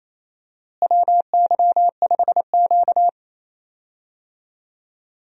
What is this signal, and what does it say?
Morse code, heard as a keyed tone, about 28 wpm, 700 Hz. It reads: WY5Q
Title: USA Call Signs - 4 Characters - Repeated 2x - 28wpm